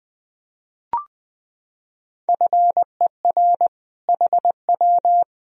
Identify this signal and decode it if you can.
Morse code, heard as a keyed tone, about 21 wpm, 700 Hz. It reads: FER HW
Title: Sentences from Top 200 Words and Abbreviations 20wpm